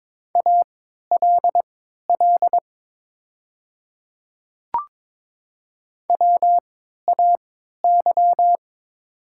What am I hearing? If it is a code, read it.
Morse code, heard as a keyed tone, about 22 wpm, spaced as for 12 wpm, 700 Hz. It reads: ALL WAY
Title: 3x Character Spacing - Top 100 Words - 22wpm